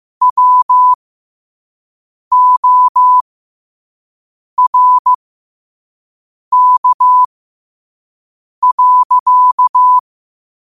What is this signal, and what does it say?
Morse code, heard as a keyed tone, about 15 wpm, spaced as for 5 wpm, 1000 Hz. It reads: WORK.
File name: 5wpm